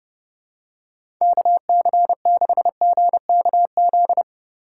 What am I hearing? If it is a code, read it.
Morse code, heard as a keyed tone, about 30 wpm, 700 Hz. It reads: KC6GKZ
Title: USA Call Signs - 6 Characters - 3x Character Spacing - 30wpm